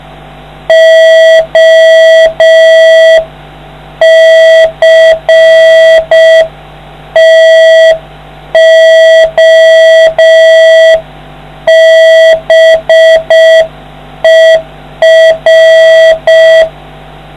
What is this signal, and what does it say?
Morse code, heard as a keyed tone, about 5 wpm, 650 Hz, with noise. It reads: OCTOBER